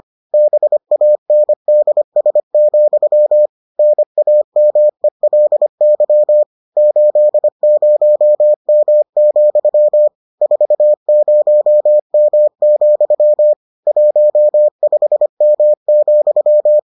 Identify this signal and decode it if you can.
Morse code, heard as a keyed tone, about 25 wpm, 600 Hz. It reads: BANDS, NAMELY 80M, 40M, 15M,